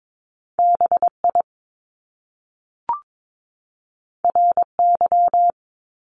Morse code, heard as a keyed tone, about 22 wpm, 700 Hz. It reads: BI RY
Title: Top Two Letter Combinations in Words 22wpm